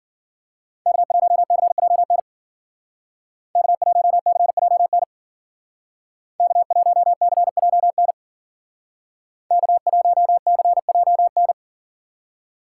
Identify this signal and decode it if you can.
Morse code, heard as a keyed tone, about 49 wpm, 700 Hz. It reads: K1CJN K1CJN K1CJN K1CJN
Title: Speed Racing - USA Call Signs 40wpm